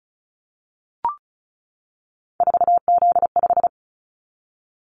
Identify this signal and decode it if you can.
Morse code, heard as a keyed tone, about 34 wpm, 700 Hz. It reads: 4Z5
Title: Callsign Prefixes for Top 50 DXCC - 35wpm